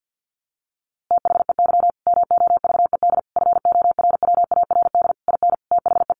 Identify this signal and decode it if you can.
Morse code, heard as a keyed tone, about 50 wpm, 700 Hz. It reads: THEY MOVED FORWARD IN THE
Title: A Sound Of Thunder 50wpm